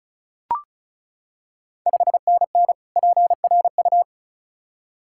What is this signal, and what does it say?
Morse code, heard as a keyed tone, about 36 wpm, 700 Hz. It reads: 5NN PRU
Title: Ontario QSO Party Exchange 35wpm